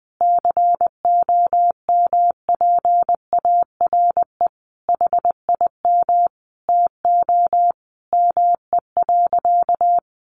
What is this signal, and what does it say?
Morse code, heard as a keyed tone, about 20 wpm, 700 Hz. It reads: COMPARE HIM TO ME.
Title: Sentences from Top 1300 Words 20wpm